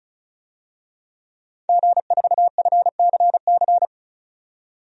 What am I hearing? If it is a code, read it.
Morse code, heard as a keyed tone, about 36 wpm, 700 Hz. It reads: G4FCC
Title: DXCC-100 Call Signs - 5 Characters - 35wpm